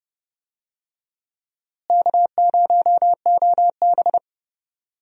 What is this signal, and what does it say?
Morse code, heard as a keyed tone, about 30 wpm, 700 Hz. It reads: K0OB
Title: USA Call Signs - 4 Characters - Repeated 3x - 30wpm